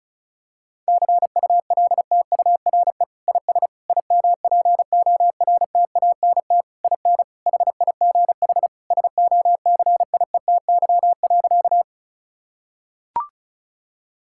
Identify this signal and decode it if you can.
Morse code, heard as a keyed tone, about 36 wpm, 700 Hz. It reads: CULTURE IS IMPORTANT IN HIGH SOCIETY.
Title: Sentences from Top 1000 Words - Review Part 6 - 35wpm